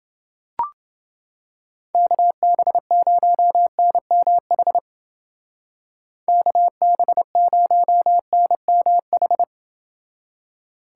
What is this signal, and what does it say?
Morse code, heard as a keyed tone, about 28 wpm, 700 Hz. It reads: KB0NMH KB0NMH